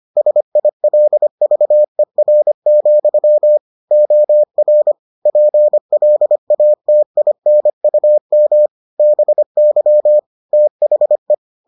Morse code, heard as a keyed tone, about 25 wpm, 600 Hz. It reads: SILVER, OR PLATINUM BY THE